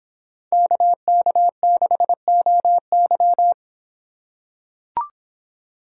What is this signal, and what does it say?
Morse code, heard as a keyed tone, about 26 wpm, 700 Hz. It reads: KK6OY